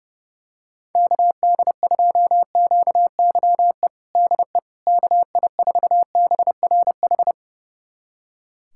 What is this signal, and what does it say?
Morse code, heard as a keyed tone, about 32 wpm, 700 Hz. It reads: KD2QYE DE KI4BRH
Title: USA Callsign DE Callsign 30wpm